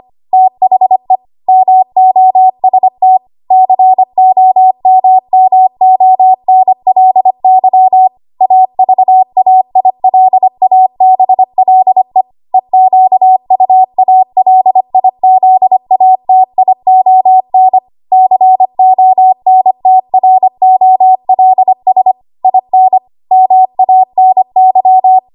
Morse code, heard as a keyed tone, about 25 wpm, 750 Hz. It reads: THE MOST COMMONLY AVAILABLE EQUALIZATION CONTROLS IN MANY
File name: Morse / 25WPM